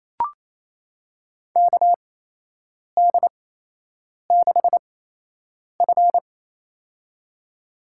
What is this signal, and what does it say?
Morse code, heard as a keyed tone, about 28 wpm, spaced as for 8 wpm, 700 Hz. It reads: KD6F
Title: USA Call Signs - 4 Characters - 8x Character Spacing - 28wpm